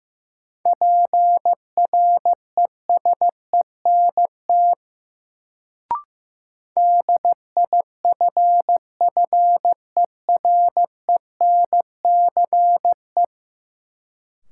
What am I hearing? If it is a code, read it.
Morse code, heard as a keyed tone, about 15 wpm, 700 Hz. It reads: PRESENT DIFFERENCE
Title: Sentences from Top 800 Words - New Word Drill - 15wpm